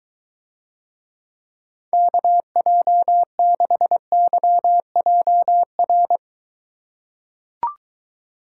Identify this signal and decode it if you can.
Morse code, heard as a keyed tone, about 23 wpm, 700 Hz. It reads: KJ6YJR